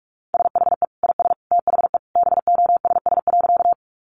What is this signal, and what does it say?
Morse code, heard as a keyed tone, about 44 wpm, 700 Hz. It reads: SHE IS THE BOSS.
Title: Sentences from Top 700 Words 45wpm